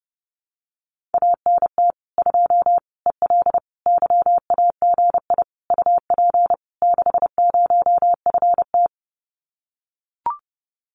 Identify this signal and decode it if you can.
Morse code, heard as a keyed tone, about 30 wpm, 700 Hz. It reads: ANT 2 EL YAGI UP 60FT
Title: Common QSO Phrases Encore 30wpm